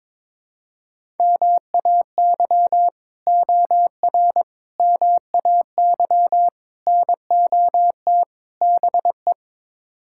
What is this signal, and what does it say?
Morse code, heard as a keyed tone, about 22 wpm, 700 Hz. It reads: MAY OR MAY NOT BE
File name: practice / 22wpm / Build A Phrase - 5-Word Phrases - 22wpm